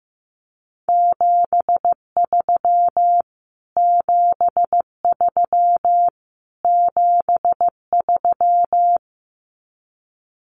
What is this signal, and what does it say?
Morse code, heard as a keyed tone, about 15 wpm, 700 Hz. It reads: 73 73 73
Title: Abbreviations Repeated 3x - 15wpm